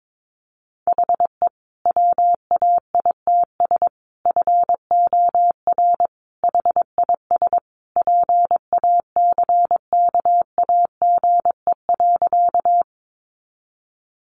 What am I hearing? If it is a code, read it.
Morse code, heard as a keyed tone, about 22 wpm, 700 Hz. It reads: HE WAITS FOR HIS PACKAGE.
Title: Additional Easy Phrases 22wpm